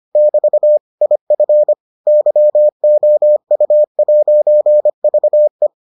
Morse code, heard as a keyed tone, about 25 wpm, 600 Hz. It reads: <BT> IF YOU'VE